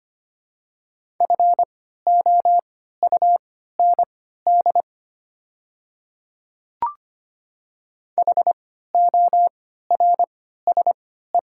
Morse code, heard as a keyed tone, about 25 wpm, spaced as for 14 wpm, 700 Hz. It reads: FOUND HORSE
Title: 3x Character Spacing - Top 500 Words - 25wpm